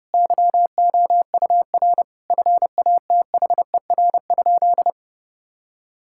Morse code, heard as a keyed tone, about 30 wpm, 700 Hz. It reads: YOUR FATHER?
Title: Sentences from Top 1000 Words - Review Part 5 - 30wpm